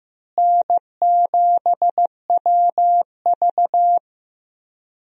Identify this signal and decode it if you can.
Morse code, heard as a keyed tone, about 15 wpm, 700 Hz. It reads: N7WV
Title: USA Call Signs - 4 Characters - Repeated 2x - 15wpm